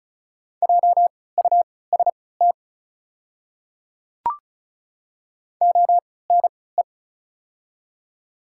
Morse code, heard as a keyed tone, about 36 wpm, spaced as for 20 wpm, 700 Hz. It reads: JUST ONE
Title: 3x Character Spacing - Top 300 Words - 35wpm